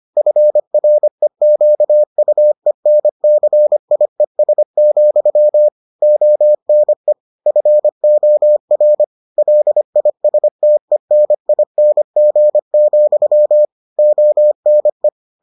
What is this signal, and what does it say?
Morse code, heard as a keyed tone, about 25 wpm, 600 Hz. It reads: FREQUENCIES, ONE FOR LISTENING, ONE